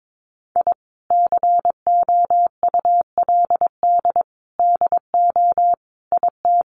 Morse code, heard as a keyed tone, about 22 wpm, 700 Hz. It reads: I COULD DO IT
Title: Sentences from Top 1300 Words 22wpm